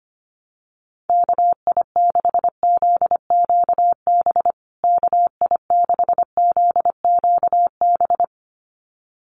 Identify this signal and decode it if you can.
Morse code, heard as a keyed tone, about 25 wpm, 700 Hz. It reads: KI6ZQB KI6ZQB
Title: USA Call Signs Repeated 2x 25wpm